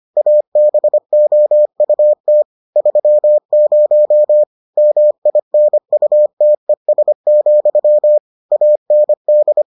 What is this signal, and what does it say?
Morse code, heard as a keyed tone, about 25 wpm, 600 Hz. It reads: ABOUT 30 MINUTES, AND